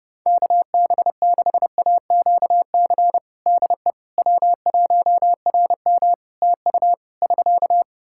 Morse code, heard as a keyed tone, about 30 wpm, 700 Hz. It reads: KB6AQC DE W1RM TU <SK>
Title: Practice QSO - Exchange 4 - Ending - 30wpm